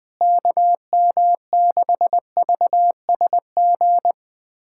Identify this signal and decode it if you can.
Morse code, heard as a keyed tone, about 20 wpm, 700 Hz. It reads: KM6VSG
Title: USA Callsign DE Callsign 20wpm